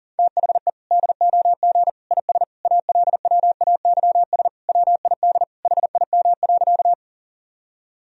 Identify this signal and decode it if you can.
Morse code, heard as a keyed tone, about 40 wpm, 700 Hz. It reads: THE DOG IS ALWAYS WID HIM.
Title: Sentences from Top 400 Words and Abbreviations 40wpm